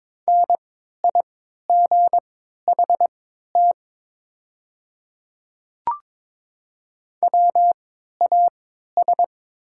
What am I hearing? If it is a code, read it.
Morse code, heard as a keyed tone, about 22 wpm, spaced as for 12 wpm, 690 Hz. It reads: NIGHT WAS
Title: 3x Character Spacing - Top 300 Words - 22wpm